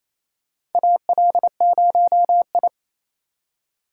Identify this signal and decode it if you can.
Morse code, heard as a keyed tone, about 28 wpm, 700 Hz. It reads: AL0I
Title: USA Call Signs - 4 Characters - 3x Character Spacing - 28wpm